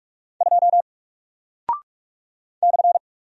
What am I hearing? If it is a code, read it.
Morse code, heard as a keyed tone, about 44 wpm, 700 Hz. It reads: 2 /
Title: Single Character (Letter:Number:Punctuation:Prosign) 45wpm